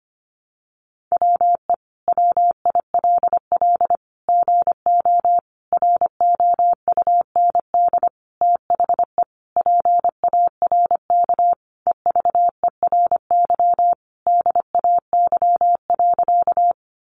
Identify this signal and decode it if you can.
Morse code, heard as a keyed tone, about 25 wpm, 700 Hz. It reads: WE WILL GO ROUND THE PARK EVERY DAY.
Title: Sentences from Top 1300 Words 25wpm